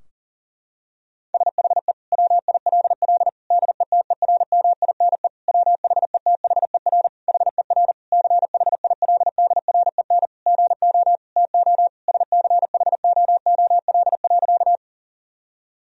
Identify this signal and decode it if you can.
Morse code, heard as a keyed tone, about 40 wpm, 700 Hz. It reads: SHE WILL DETERMINE WHETHER HER CHILDREN GO TO SCHOOL.
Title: Sentences from Top 1000 Words - Review Part 4 - 40wpm